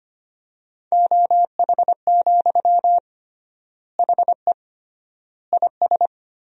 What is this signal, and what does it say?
Morse code, heard as a keyed tone, about 25 wpm, 700 Hz. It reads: OH, HE IS
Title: 3x Word-Spacing - Sentences from Top 500 Words - 25wpm